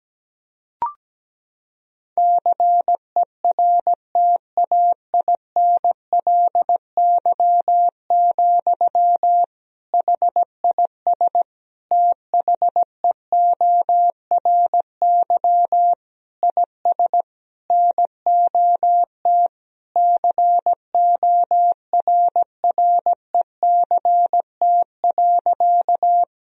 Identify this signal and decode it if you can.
Morse code, heard as a keyed tone, about 17 wpm, 700 Hz. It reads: CERTAINLY, HIS THEORY IS NOT CORRECT.
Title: Sentences from Top 1000 Words - Review Part 3 - 17wpm